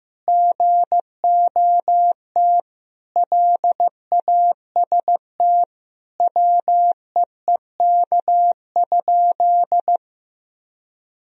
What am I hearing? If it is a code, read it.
Morse code, heard as a keyed tone, about 15 wpm, 700 Hz. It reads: GOT LAST WEEK?
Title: Sentences from Top 1000 Words - Review Part 3 - 15wpm